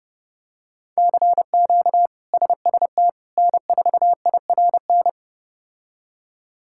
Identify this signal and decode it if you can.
Morse code, heard as a keyed tone, about 30 wpm, 700 Hz. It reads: CQ SST N4IRN
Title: K1USN SST QSO Practice - 30wpm